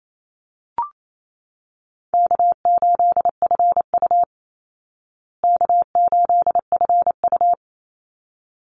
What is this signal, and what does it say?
Morse code, heard as a keyed tone, about 28 wpm, 700 Hz. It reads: K8FU K8FU
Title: Easy Call Signs Repeated 2x - 28wpm